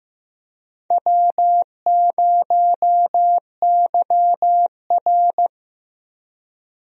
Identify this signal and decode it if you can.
Morse code, heard as a keyed tone, about 15 wpm, 700 Hz. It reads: W0YR
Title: USA Call Signs - 4 Characters - Repeated 2x - 15wpm